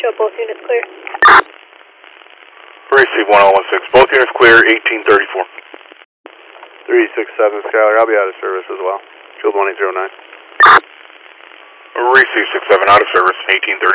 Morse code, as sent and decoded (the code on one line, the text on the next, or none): none